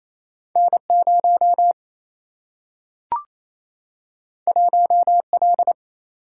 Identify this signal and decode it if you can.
Morse code, heard as a keyed tone, about 28 wpm, 700 Hz. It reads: N 0 1 L
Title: Single-Letter Single-Number 28wpm